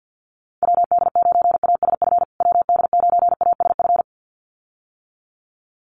Morse code, heard as a keyed tone, about 50 wpm, 700 Hz. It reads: WD9ASF WD9ASF
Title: USA Call Signs Repeated 2x 50wpm